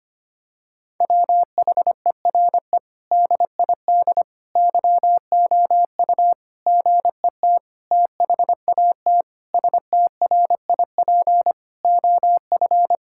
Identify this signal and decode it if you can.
Morse code, heard as a keyed tone, about 25 wpm, 700 Hz. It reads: WHERE DID YOU GET THAT STRIP OF
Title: Sentences from Top 1600 Words 25wpm